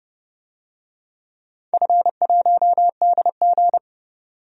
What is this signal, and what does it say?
Morse code, heard as a keyed tone, about 30 wpm, 700 Hz. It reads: F1DG
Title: DXCC-100 Call Signs - 4 Characters - 30wpm